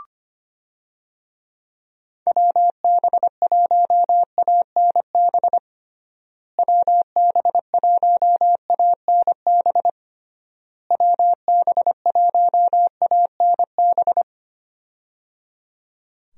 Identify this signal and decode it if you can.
Morse code, heard as a keyed tone, about 25 wpm, 700 Hz. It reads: WB1ANB WB1ANB WB1ANB
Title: USA Call Signs - 6 Characters - Repeated 3x - 25wpm